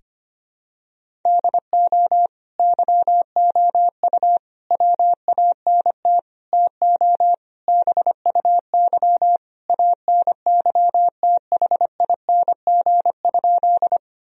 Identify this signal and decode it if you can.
Morse code, heard as a keyed tone, about 25 wpm, 700 Hz. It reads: DO YOU WANT TO BUY ANYTHING?
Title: Sentences from Top 1000 Words - Review Part 4 - 25wpm